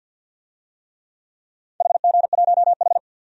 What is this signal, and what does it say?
Morse code, heard as a keyed tone, about 50 wpm, 700 Hz. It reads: HG1H
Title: DXCC-100 Call Signs - 4 Characters - 50wpm